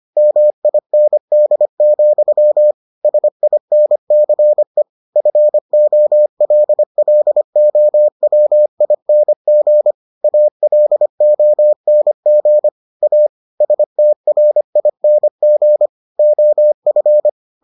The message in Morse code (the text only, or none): MIND, SINCE FOLLOWING ALONG A STRING OF